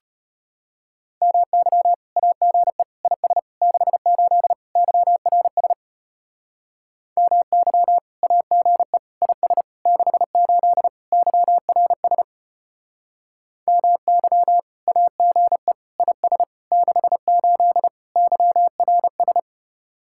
Morse code, heard as a keyed tone, about 32 wpm, 700 Hz. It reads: MY AGE IS 68 YRS MY AGE IS 68 YRS MY AGE IS 68 YRS